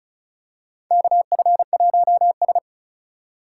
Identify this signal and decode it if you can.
Morse code, heard as a keyed tone, about 34 wpm, 700 Hz. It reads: KF1S
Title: USA Call Signs - 2x Character Spacing - 35wpm